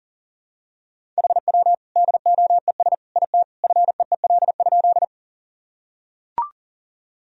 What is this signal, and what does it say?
Morse code, heard as a keyed tone, about 40 wpm, 700 Hz. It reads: HW DOES IT FEEL?
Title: Sentences from Top 400 Words and Abbreviations 40wpm